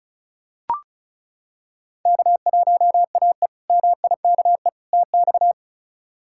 Morse code, heard as a keyed tone, about 36 wpm, 700 Hz. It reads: K1AE MIKE TX
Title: K1USN SST QSO Practice - 35wpm